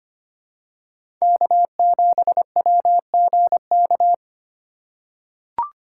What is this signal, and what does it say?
Morse code, heard as a keyed tone, about 25 wpm, 700 Hz. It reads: K7WGK
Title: USA Call Signs - 5 Characters - 2x Character Spacing - 25wpm